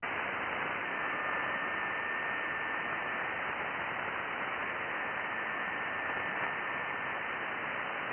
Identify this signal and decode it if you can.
FT8.
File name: IC-7100